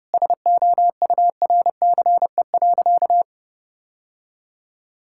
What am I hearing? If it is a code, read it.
Morse code, heard as a keyed tone, about 31 wpm, 700 Hz. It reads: SOURCE.
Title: Sentences from Top 1000 Words - Review Part 5 - 30wpm